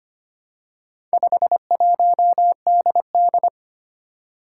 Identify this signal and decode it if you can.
Morse code, heard as a keyed tone, about 25 wpm, 700 Hz. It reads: 51DD
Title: Four Challenging Letter-Number Combinations 25wpm